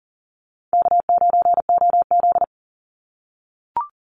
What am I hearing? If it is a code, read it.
Morse code, heard as a keyed tone, about 40 wpm, 700 Hz. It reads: K9OZ
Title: NA SOTA Chaser Callsigns 40wpm